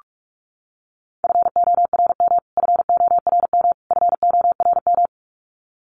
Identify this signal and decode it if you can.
Morse code, heard as a keyed tone, about 44 wpm, 700 Hz. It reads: FORM FORM FORM
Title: Top 500 Words Repeated 45wpm